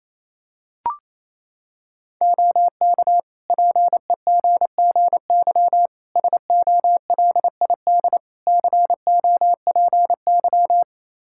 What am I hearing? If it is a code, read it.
Morse code, heard as a keyed tone, about 28 wpm, 700 Hz. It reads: OK PEGGY SOLID COPY
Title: Practice QSO - Exchange 3 - More Info - 28wpm